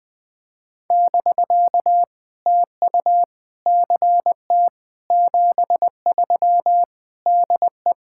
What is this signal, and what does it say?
Morse code, heard as a keyed tone, about 20 wpm, 700 Hz. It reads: <BK> TU CT 73 DE
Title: POTA QSO Practice - 20wpm